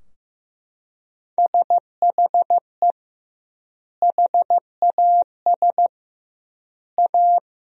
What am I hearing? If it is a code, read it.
Morse code, heard as a keyed tone, about 16 wpm, 700 Hz. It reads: SHE HAS A
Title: 2x Word-Spacing - Sentences from Top 200 Words - 15wpm